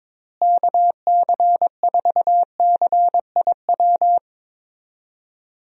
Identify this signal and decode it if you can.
Morse code, heard as a keyed tone, about 22 wpm, 700 Hz. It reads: KC4CIW
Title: USA Call Signs - 6 Characters - Repeated 2x - 22wpm